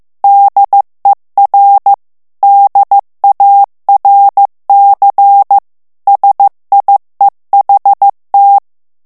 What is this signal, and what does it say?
Morse code, heard as a keyed tone, about 15 wpm, 800 Hz. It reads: DER DARC SIEHT
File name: morse mp3 / ALLEMAND texte clair 1 vitesse 14 mots minute